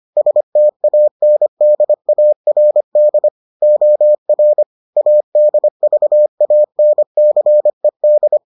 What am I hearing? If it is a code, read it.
Morse code, heard as a keyed tone, about 25 wpm, 600 Hz. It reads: STANDARD OR ADVANCED